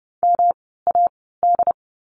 Morse code, heard as a keyed tone, about 30 wpm, spaced as for 17 wpm, 700 Hz. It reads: MAD
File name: course / 30:10wpm / 108 Lesson 18f - Letter m - Words 30wpm